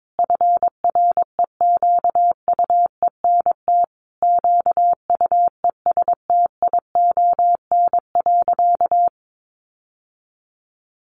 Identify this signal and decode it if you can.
Morse code, heard as a keyed tone, about 22 wpm, 700 Hz. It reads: FREQUENT QUESTION.
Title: Sentences from Top 1400 Words 22wpm